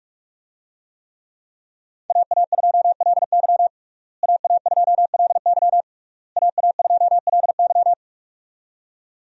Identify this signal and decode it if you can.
Morse code, heard as a keyed tone, about 46 wpm, 700 Hz. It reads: AA2LY AA2LY AA2LY
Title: USA Call Signs - 5 Characters - Repeated 3x - 45wpm